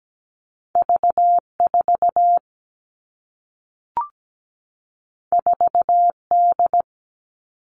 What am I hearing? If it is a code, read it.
Morse code, heard as a keyed tone, about 17 wpm, 700 Hz. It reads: V4 4D
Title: All Cut Numbers - 2-Digit 17wpm